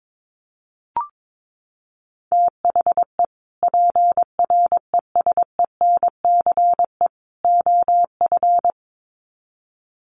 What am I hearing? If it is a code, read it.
Morse code, heard as a keyed tone, about 22 wpm, 700 Hz. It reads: THE PRESENCE OF
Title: Sets of 3 Words 22wpm